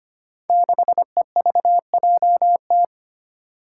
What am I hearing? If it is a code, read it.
Morse code, heard as a keyed tone, about 25 wpm, 700 Hz. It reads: 6EVJT
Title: Five Challenging Letter-Number Combinations 25wpm